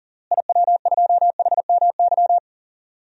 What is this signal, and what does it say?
Morse code, heard as a keyed tone, about 40 wpm, 700 Hz. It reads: IW2HMY
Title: DXCC-100 Call Signs - 6 Characters - Repeated 2x - 40wpm